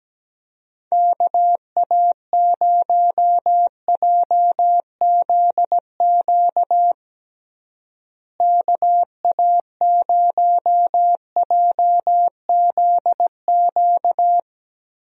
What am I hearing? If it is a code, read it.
Morse code, heard as a keyed tone, about 17 wpm, 700 Hz. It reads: KA0JZQ KA0JZQ
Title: USA Call Signs - 6 Characters - Repeated 3x - 17wpm